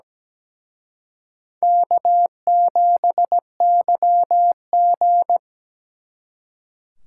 Morse code, heard as a keyed tone, about 20 wpm, 700 Hz. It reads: K7YG
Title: Speed Racing - USA 1x2 Call Signs 17wpm